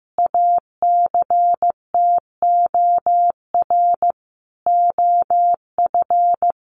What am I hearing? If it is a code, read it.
Morse code, heard as a keyed tone, about 14 wpm, 700 Hz. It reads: ACTOR OF
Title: Sentences from Top 1500 Words 15wpm